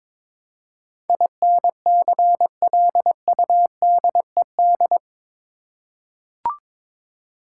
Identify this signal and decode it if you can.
Morse code, heard as a keyed tone, about 22 wpm, 700 Hz. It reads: INCLUDED
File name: practice / 22wpm / Eight Letter Words - 22wpm